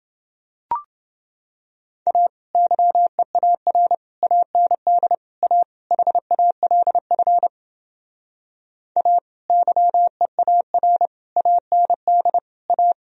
Morse code, heard as a keyed tone, about 31 wpm, 700 Hz. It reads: A YEAR AND A HALF A YEAR AND A